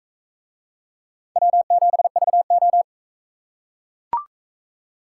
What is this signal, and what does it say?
Morse code, heard as a keyed tone, about 38 wpm, 700 Hz. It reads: W7VO